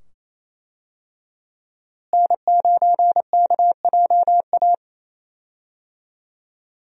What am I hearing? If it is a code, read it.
Morse code, heard as a keyed tone, about 28 wpm, 700 Hz. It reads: N9KJA
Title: USA Call Signs - 5 Characters - Repeated 2x - 28wpm